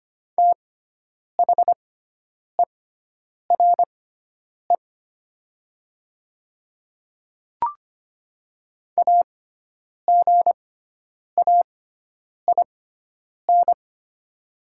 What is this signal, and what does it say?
Morse code, heard as a keyed tone, about 25 wpm, spaced as for 9 wpm, 700 Hz. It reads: THERE AGAIN
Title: 6x Character Spacing - Top 200 Words - 25wpm